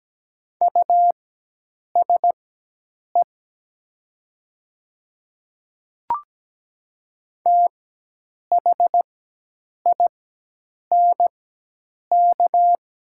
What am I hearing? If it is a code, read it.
Morse code, heard as a keyed tone, about 17 wpm, spaced as for 8 wpm, 690 Hz. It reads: USE THINK
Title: 4x Character Spacing - Top 200 Words - 17wpm